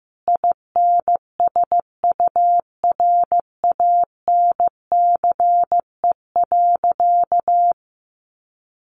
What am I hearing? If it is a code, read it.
Morse code, heard as a keyed tone, about 15 wpm, 700 Hz. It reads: INSURANCE.
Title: A Sound Of Thunder 15wpm